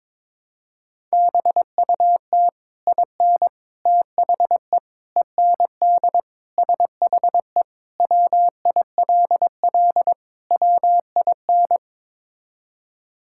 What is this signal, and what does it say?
Morse code, heard as a keyed tone, about 22 wpm, 700 Hz. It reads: BUT IN THE END SHE WILL WIN
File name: practice / 22wpm / Sentences with 3-Word Phrases - 22wpm